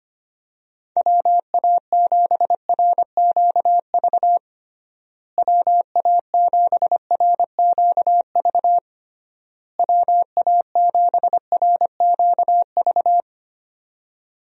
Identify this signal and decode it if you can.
Morse code, heard as a keyed tone, about 25 wpm, 700 Hz. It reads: WA7RQV WA7RQV WA7RQV
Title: USA Call Signs - 6 Characters - Repeated 3x - 25wpm